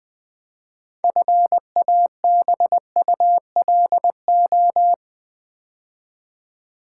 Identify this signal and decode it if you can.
Morse code, heard as a keyed tone, about 20 wpm, 690 Hz. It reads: FABULO
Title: Build A Word - 3 Letters At A Time - 8-Letter Words - 20wpm